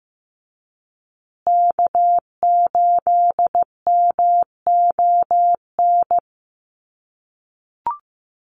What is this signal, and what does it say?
Morse code, heard as a keyed tone, about 15 wpm, 700 Hz. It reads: K8MON
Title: USA Call Signs Repeated 2x 15wpm